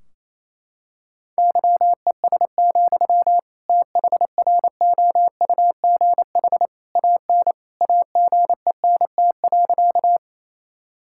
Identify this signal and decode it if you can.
Morse code, heard as a keyed tone, about 28 wpm, 700 Hz. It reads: YES, THROUGH AN AGENT.
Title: Sentences from Top 1600 Words 28wpm